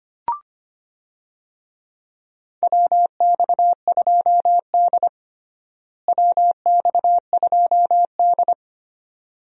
Morse code, heard as a keyed tone, about 25 wpm, 700 Hz. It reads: WX2D WX2D
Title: USA Call Signs - 4 Characters - Repeated 2x - 25wpm